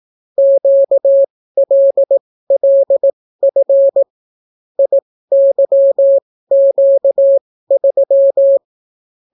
Morse code, heard as a keyed tone, about 18 wpm, spaced as for 14 wpm, 550 Hz. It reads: QLLF IYQ3